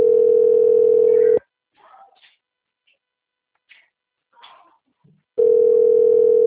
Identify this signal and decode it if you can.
Ringback tone (North America), 440 + 480 Hz.